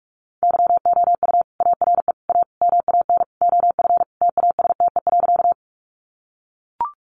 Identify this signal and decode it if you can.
Morse code, heard as a keyed tone, about 44 wpm, 700 Hz. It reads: YOU ARE A MAN OF TASTE.
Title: Sentences from Top 1100 Words 45wpm